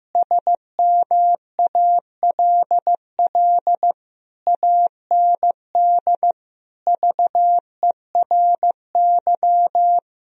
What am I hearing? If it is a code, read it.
Morse code, heard as a keyed tone, about 15 wpm, 700 Hz. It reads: SMALL AND VERY
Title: Sentences from Top 1000 Words - Review Part 1 - 15wpm